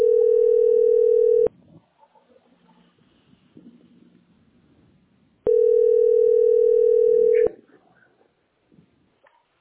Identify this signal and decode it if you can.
Ringback tone (North America), 440 + 480 Hz.